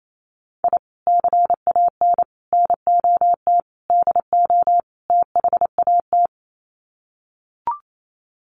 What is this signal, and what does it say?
Morse code, heard as a keyed tone, about 28 wpm, 700 Hz. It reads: I CAN NOT DO THAT
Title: Sets of 5 Words Encore2 28wpm